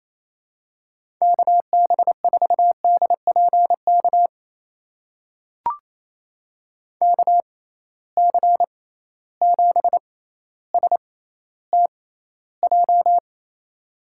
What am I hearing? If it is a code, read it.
Morse code, heard as a keyed tone, about 28 wpm, spaced as for 14 wpm, 700 Hz. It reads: KB4DPK KC7STJ
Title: USA Call Signs - 6 Characters - 6x Character Spacing - 28wpm